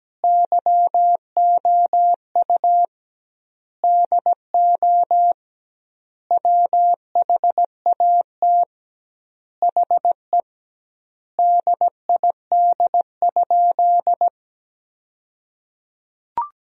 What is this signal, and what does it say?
Morse code, heard as a keyed tone, about 17 wpm, 700 Hz. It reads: YOU DO WHAT HE DID?
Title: 2x Word-Spacing - Sentences from Top 100 Words - 17wpm